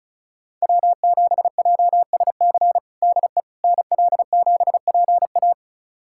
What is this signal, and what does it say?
Morse code, heard as a keyed tone, about 34 wpm, 700 Hz. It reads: W7JSC DE NL7PA
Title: USA Callsign DE Callsign - 5 Characters - 35wpm